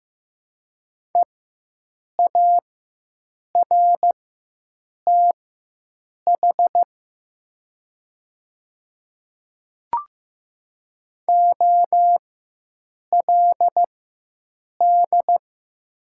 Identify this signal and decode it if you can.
Morse code, heard as a keyed tone, about 15 wpm, spaced as for 7 wpm, 700 Hz. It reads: EARTH OLD
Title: 4x Character Spacing - Top 200 Words - 15wpm